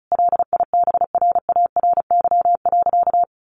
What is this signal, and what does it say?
Morse code, heard as a keyed tone, about 34 wpm, 700 Hz. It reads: LIBRARY.
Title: Sentences from Top 600 Words 35wpm